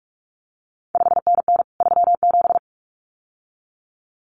Morse code, heard as a keyed tone, about 46 wpm, 700 Hz. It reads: 5NN 37
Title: CQ WW DX Contest Exchange Encore - 45wpm